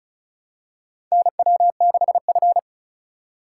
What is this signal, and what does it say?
Morse code, heard as a keyed tone, about 36 wpm, 700 Hz. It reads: NW6F
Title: USA Call Signs - 4 Characters - 6x Character Spacing - 35wpm